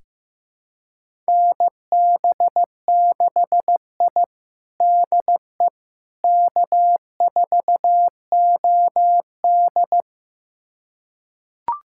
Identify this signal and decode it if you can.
Morse code, heard as a keyed tone, about 15 wpm, 700 Hz. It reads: NB6I DE K4OD
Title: USA Callsign DE Callsign - 4 Characters - 15wpm